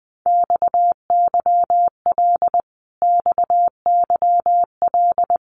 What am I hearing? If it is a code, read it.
Morse code, heard as a keyed tone, about 20 wpm, 700 Hz. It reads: XYL XYL
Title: Abbreviations Repeated 3x - 20wpm